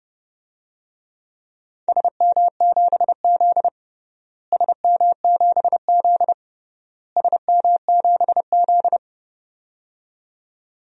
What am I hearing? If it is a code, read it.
Morse code, heard as a keyed tone, about 31 wpm, 700 Hz. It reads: SM7Z SM7Z SM7Z
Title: DXCC-100 Call Signs - 4 Characters - Repeated 3x - 30wpm